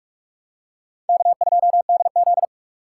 Morse code, heard as a keyed tone, about 41 wpm, 700 Hz. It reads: K2DZ
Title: USA Call Signs - 2x Character Spacing - 45wpm